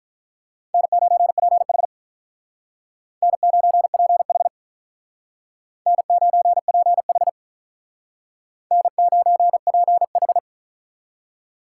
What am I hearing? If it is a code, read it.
Morse code, heard as a keyed tone, about 44 wpm, 700 Hz. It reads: N9PH N9PH N9PH N9PH